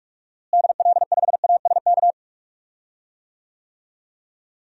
Morse code, heard as a keyed tone, about 41 wpm, 700 Hz. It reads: DL5ASK
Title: DXCC-100 Call Signs - 6 Characters - Repeated 3x - 45wpm